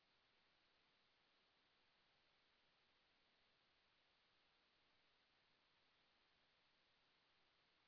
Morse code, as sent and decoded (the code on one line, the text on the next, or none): none